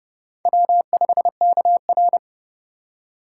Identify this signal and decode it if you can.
Morse code, heard as a keyed tone, about 29 wpm, 700 Hz. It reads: W5KR